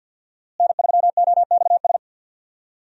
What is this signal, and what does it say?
Morse code, heard as a keyed tone, about 49 wpm, 700 Hz. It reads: N3OXS
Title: USA Call Signs - 5 Characters - Repeated 2x - 50wpm